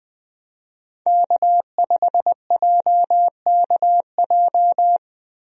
Morse code, heard as a keyed tone, about 20 wpm, 700 Hz. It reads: K5JKJ
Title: USA Call Signs - 5 Characters - Repeated 3x - 20wpm